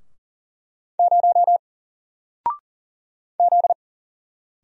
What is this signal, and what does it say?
Morse code, heard as a keyed tone, about 40 wpm, 700 Hz. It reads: 0 Z
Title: Single Letter-Number Alt 40wpm